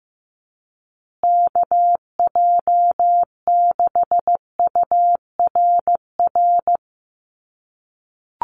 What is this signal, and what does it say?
Morse code, heard as a keyed tone, about 15 wpm, 700 Hz. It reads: KJ6URR
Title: USA Call Signs Repeated 2x 15wpm